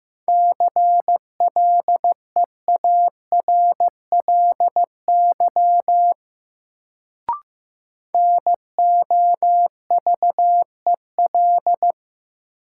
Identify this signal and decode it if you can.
Morse code, heard as a keyed tone, about 15 wpm, 700 Hz. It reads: CLEARLY NOVEL
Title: Sentences from Top 1300 Words - New Word Drill - 15wpm